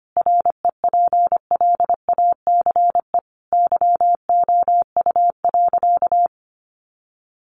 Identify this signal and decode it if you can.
Morse code, heard as a keyed tone, about 25 wpm, 700 Hz. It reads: REPLACE YOU.
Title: Sentences from Top 1400 Words 25wpm